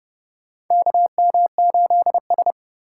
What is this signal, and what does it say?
Morse code, heard as a keyed tone, about 30 wpm, 700 Hz. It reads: KM8S
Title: USA Call Signs - 2x Character Spacing - 30wpm